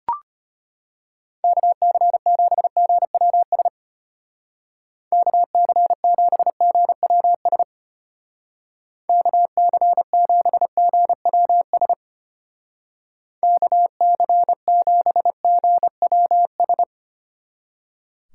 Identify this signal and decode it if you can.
Morse code, heard as a keyed tone, about 32 wpm, 700 Hz. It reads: KC7GWS KC7GWS KC7GWS KC7GWS